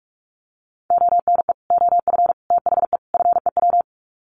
Morse code, heard as a keyed tone, about 44 wpm, 700 Hz. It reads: ONE OF THE FEW
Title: Sets of 4 Words 45wpm